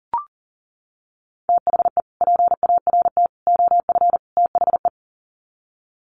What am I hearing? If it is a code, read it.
Morse code, heard as a keyed tone, about 40 wpm, 700 Hz. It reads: THE PART OF THE
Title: Sets of 4 Words 40wpm